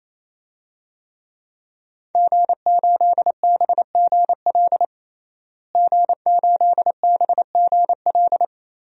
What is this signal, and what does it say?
Morse code, heard as a keyed tone, about 27 wpm, 700 Hz. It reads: G8BGL G8BGL
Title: DXCC-100 Call Signs - 5 Characters - Repeated 3x - 28wpm